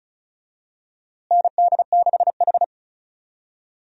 Morse code, heard as a keyed tone, about 36 wpm, 700 Hz. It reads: ND6H
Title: USA Call Signs - 4 Characters - Repeated 2x - 35wpm